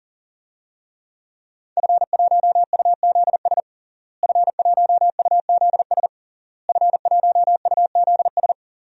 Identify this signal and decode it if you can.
Morse code, heard as a keyed tone, about 40 wpm, 700 Hz. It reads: F1UZS F1UZS F1UZS
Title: DXCC-100 Call Signs - 5 Characters - Repeated 3x - 40wpm